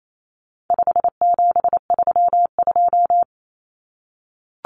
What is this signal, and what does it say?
Morse code, heard as a keyed tone, about 28 wpm, 700 Hz. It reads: 5732
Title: Numbers - 4-Digit 28wpm